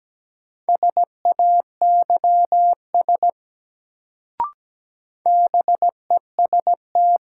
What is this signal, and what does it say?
Morse code, heard as a keyed tone, about 17 wpm, 700 Hz. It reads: SAYS BEST
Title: Sets of 2 Words - New Word Drill - 17wpm